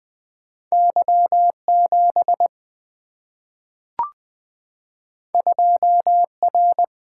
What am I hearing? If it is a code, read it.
Morse code, heard as a keyed tone, about 20 wpm, 700 Hz. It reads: Y7 2R
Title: Two Letter-Number Combinations - 20wpm